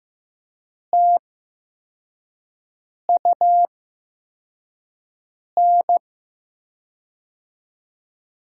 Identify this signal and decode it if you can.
Morse code, heard as a keyed tone, about 15 wpm, spaced as for 4 wpm, 700 Hz. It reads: TUN
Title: USA Call Signs - 5 Characters - 8x Character Spacing - 15wpm